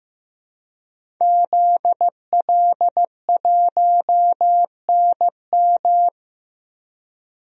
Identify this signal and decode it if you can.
Morse code, heard as a keyed tone, about 15 wpm, 700 Hz. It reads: ZL1NM
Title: DXCC-100 Call Signs - 5 Characters - 15wpm